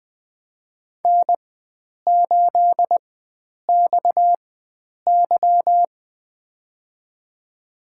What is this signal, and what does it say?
Morse code, heard as a keyed tone, about 20 wpm, spaced as for 9 wpm, 700 Hz. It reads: N8XY
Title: USA Call Signs - 4x Character Spacing - 20wpm